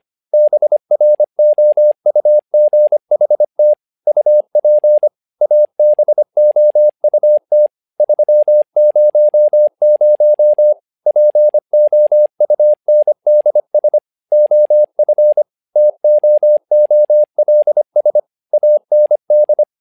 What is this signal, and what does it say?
Morse code, heard as a keyed tone, about 25 wpm, 600 Hz. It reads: BROUGHT UP ABOUT 300 POUNDS OF TOOLS AND
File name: learning center / tower trouble / tower trouble 25wpm